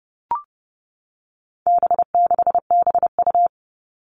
Morse code, heard as a keyed tone, about 28 wpm, 700 Hz. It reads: B6BU
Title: All Cut Numbers - 4-Digit 30wpm